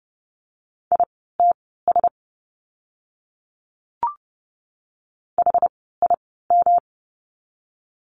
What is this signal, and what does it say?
Morse code, heard as a keyed tone, about 30 wpm, spaced as for 17 wpm, 700 Hz. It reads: ITS HIM
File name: course / 30:10wpm / 237 Lesson 38f - Letter q - Words 30wpm